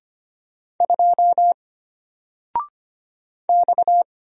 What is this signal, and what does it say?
Morse code, heard as a keyed tone, about 25 wpm, 700 Hz. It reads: 2 X
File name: Single-Letter-Number-25wpm